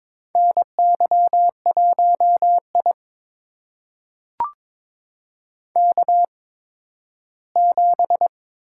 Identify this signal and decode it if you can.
Morse code, heard as a keyed tone, about 22 wpm, spaced as for 13 wpm, 700 Hz. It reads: NY1I K7
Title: USA Call Signs - 4 Characters - 8x Character Spacing - 22wpm